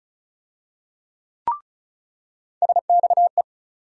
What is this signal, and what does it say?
Morse code, heard as a keyed tone, about 34 wpm, 700 Hz. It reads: SXE
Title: USA Call Sign Postfixes - 3 Letters 35wpm